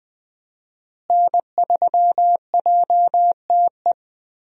Morse code, heard as a keyed tone, about 20 wpm, 700 Hz. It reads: N3JTE
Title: USA Call Signs - 5 Characters - Repeated 3x - 20wpm